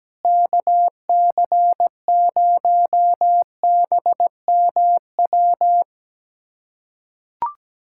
Morse code, heard as a keyed tone, about 17 wpm, 700 Hz. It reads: KC0BMW
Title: USA Callsign DE Callsign - 6 Characters - 17wpm